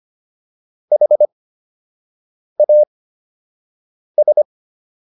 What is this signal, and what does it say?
Morse code, heard as a keyed tone, about 25 wpm, spaced as for 6 wpm, 600 Hz. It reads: HAS